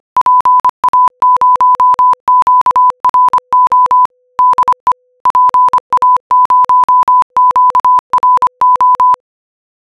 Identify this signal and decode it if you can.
Morse code, heard as a keyed tone, about 25 wpm, 1000 Hz. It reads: PA0QRO DE PA0QRO